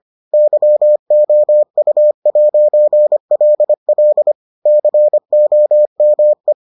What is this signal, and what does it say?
Morse code, heard as a keyed tone, about 25 wpm, 600 Hz. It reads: YOU'LL COME